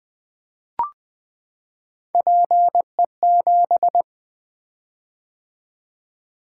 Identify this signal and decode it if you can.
Morse code, heard as a keyed tone, about 20 wpm, 700 Hz. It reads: PE7
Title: Three Letter-Number Combinations - 20wpm